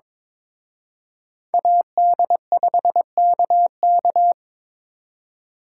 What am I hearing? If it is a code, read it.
Morse code, heard as a keyed tone, about 26 wpm, 700 Hz. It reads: AD5KK
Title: Speed Racing - USA 2x2 Call Signs 22wpm